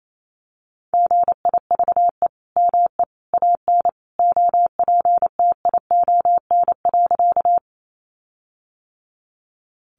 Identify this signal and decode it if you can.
Morse code, heard as a keyed tone, about 28 wpm, 700 Hz. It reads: GIVE ME AN OPTION.
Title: Sentences from Top 1300 Words 28wpm